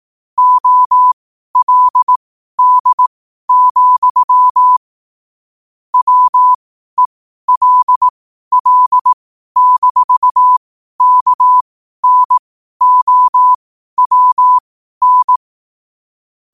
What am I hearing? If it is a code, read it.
Morse code, heard as a keyed tone, about 18 wpm, spaced as for 12 wpm, 1000 Hz. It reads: OLD, WELL-KNOWN
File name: qotdfiles / 12wpm